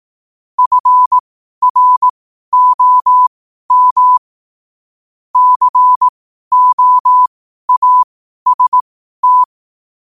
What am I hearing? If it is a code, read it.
Morse code, heard as a keyed tone, about 18 wpm, spaced as for 12 wpm, 1000 Hz. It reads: FROM COAST
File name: qotdfiles / 12wpm